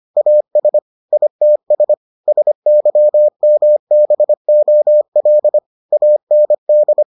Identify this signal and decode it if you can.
Morse code, heard as a keyed tone, about 25 wpm, 600 Hz. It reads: AS ITS SYMBOL AND